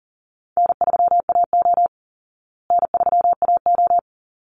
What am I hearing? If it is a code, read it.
Morse code, heard as a keyed tone, about 40 wpm, 700 Hz. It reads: N3AO N3AO
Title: Easy Call Signs Repeated 2x - 40wpm